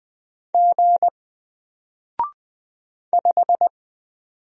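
Morse code, heard as a keyed tone, about 20 wpm, 700 Hz. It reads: G 5
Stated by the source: Single letter and number Morse drill at 20 wpm, from N4LCD.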